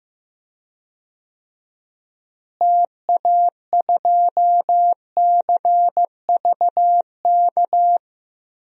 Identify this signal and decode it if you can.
Morse code, heard as a keyed tone, about 15 wpm, 700 Hz. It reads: TA2CVK
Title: DXCC-100 Call Signs - 6 Characters - Repeated 3x - 15wpm